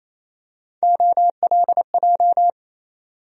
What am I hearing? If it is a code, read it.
Morse code, heard as a keyed tone, about 28 wpm, 700 Hz. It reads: OLJ
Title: USA Call Sign Postfixes - 3 Letters 28wpm